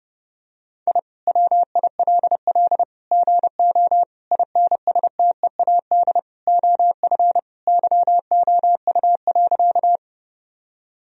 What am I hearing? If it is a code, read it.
Morse code, heard as a keyed tone, about 30 wpm, 700 Hz. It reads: I WILL GO INSTEAD OF YOU.
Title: Sentences from Top 1000 Words - Review Part 4 - 30wpm